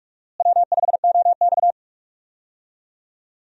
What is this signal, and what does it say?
Morse code, heard as a keyed tone, about 44 wpm, 700 Hz. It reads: W5OX
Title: USA Call Signs - 4 Characters - Repeated 2x - 45wpm